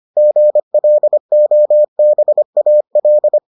Morse code, heard as a keyed tone, about 25 wpm, 600 Hz. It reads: GLOBAL